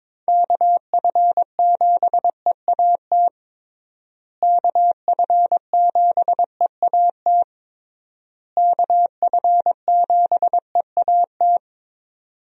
Morse code, heard as a keyed tone, about 22 wpm, 700 Hz. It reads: KF7EAT KF7EAT KF7EAT
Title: USA Call Signs - 6 Characters - Repeated 3x - 22wpm